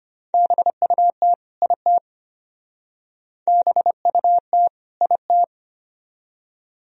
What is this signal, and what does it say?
Morse code, heard as a keyed tone, about 33 wpm, 700 Hz. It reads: BUT IT BUT IT